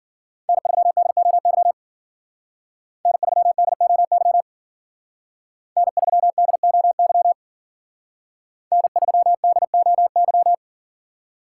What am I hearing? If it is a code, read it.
Morse code, heard as a keyed tone, about 52 wpm, 700 Hz. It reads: N3DOY N3DOY N3DOY N3DOY